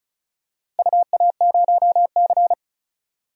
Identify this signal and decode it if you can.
Morse code, heard as a keyed tone, about 36 wpm, 700 Hz. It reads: UA0C
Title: DXCC-100 Call Signs - 4 Characters - Repeated 2x - 35wpm